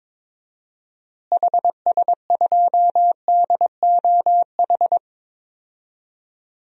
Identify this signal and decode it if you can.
Morse code, heard as a keyed tone, about 22 wpm, 700 Hz. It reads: HS2DOH
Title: DXCC-100 Call Signs - 6 Characters - Repeated 2x - 22wpm